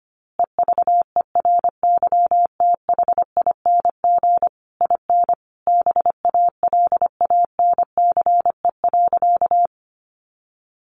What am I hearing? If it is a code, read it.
Morse code, heard as a keyed tone, about 25 wpm, 700 Hz. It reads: EVERYTHING IN BALANCE.
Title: A Sound Of Thunder 25wpm